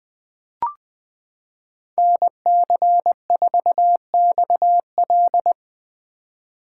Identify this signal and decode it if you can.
Morse code, heard as a keyed tone, about 20 wpm, 700 Hz. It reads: NC4XL
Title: 2023 POTA Activator Call Signs - 20wpm